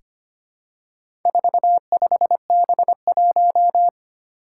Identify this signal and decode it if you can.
Morse code, heard as a keyed tone, about 25 wpm, 700 Hz. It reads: 45B1
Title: Four Challenging Letter-Number Combinations 25wpm